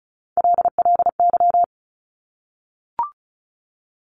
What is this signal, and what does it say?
Morse code, heard as a keyed tone, about 36 wpm, 700 Hz. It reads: LLY
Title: Top Three Letter Combinations in Words 35wpm